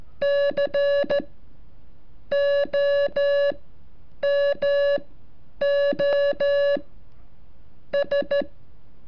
Morse code, heard as a keyed tone, about 13 wpm, spaced as for 7 wpm, 570 Hz. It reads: COMOS
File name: curso / ejercicio 34